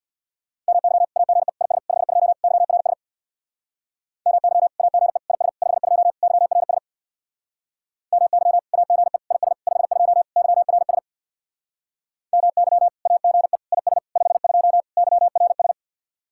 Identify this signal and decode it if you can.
Morse code, heard as a keyed tone, about 61 wpm, 700 Hz. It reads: MY AGE IS 52 YRS MY AGE IS 52 YRS MY AGE IS 52 YRS MY AGE IS 52 YRS
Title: Speed-Racing Common QSO Phrases - 50wpm